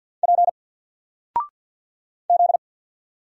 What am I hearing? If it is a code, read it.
Morse code, heard as a keyed tone, about 50 wpm, 700 Hz. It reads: P Z
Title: Single Letter-Number 50wpm